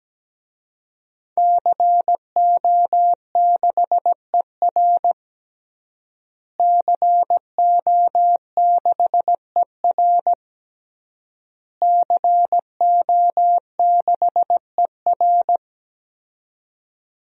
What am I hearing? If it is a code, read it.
Morse code, heard as a keyed tone, about 17 wpm, 700 Hz. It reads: CO6ER CO6ER CO6ER
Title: DXCC-100 Call Signs - 5 Characters - Repeated 3x - 17wpm